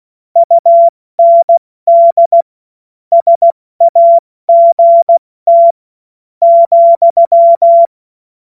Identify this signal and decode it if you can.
Morse code, heard as a keyed tone, about 16 wpm, 680 Hz. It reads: UND SAGT ,
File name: Morsekurs / Witze / Witz 24 Juni Mai 2025 70 F80